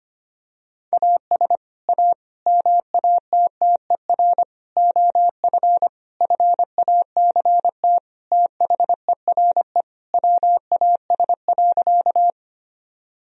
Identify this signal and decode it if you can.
Morse code, heard as a keyed tone, about 25 wpm, 700 Hz. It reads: AS A MATTER OF FACT THERE WAS.
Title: Sentences from Top 1000 Words - Review Part 3 - 25wpm